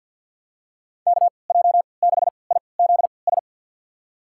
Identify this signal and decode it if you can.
Morse code, heard as a keyed tone, about 49 wpm, spaced as for 28 wpm, 700 Hz. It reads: KJ6IZS
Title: USA Call Signs - 3x Character Spacing - 50wpm